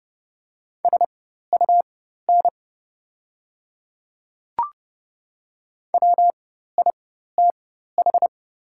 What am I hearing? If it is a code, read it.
Morse code, heard as a keyed tone, about 30 wpm, spaced as for 14 wpm, 700 Hz. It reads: SUN WITH
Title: 4x Character Spacing - Top 300 Words - 30wpm